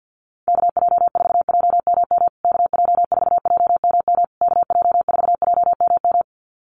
Morse code, heard as a keyed tone, about 53 wpm, 700 Hz. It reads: KJ4JMM KJ4JMM KJ4JMM
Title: USA Call Signs Repeated 3x 50wpm